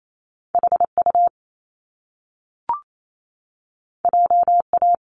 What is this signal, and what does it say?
Morse code, heard as a keyed tone, about 28 wpm, 700 Hz. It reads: HU JA
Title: Top Two Letter Combinations in Words 28wpm